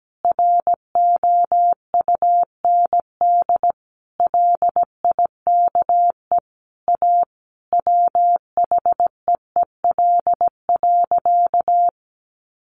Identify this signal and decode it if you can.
Morse code, heard as a keyed tone, about 17 wpm, 700 Hz. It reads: ROUND LIKE A WHEEL.
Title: Sentences from Top 500 Words 17wpm